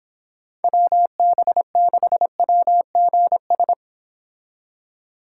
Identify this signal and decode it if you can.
Morse code, heard as a keyed tone, about 26 wpm, 700 Hz. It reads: WB6WGS